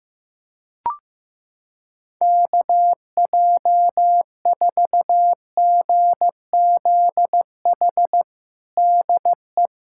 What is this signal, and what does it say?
Morse code, heard as a keyed tone, about 15 wpm, 700 Hz. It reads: KJ4GZH DE
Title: USA Callsign DE Callsign - 6 Characters - 15wpm